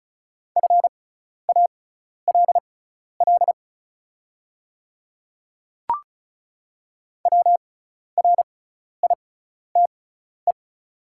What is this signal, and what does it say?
Morse code, heard as a keyed tone, about 36 wpm, spaced as for 12 wpm, 690 Hz. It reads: FALL WRITE
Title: 6x Character Spacing - Top 500 Words - 35wpm